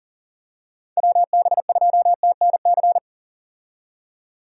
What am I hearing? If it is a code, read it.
Morse code, heard as a keyed tone, about 40 wpm, 700 Hz. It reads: WB2TNC
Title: USA Call Signs - 6 Characters - 40wpm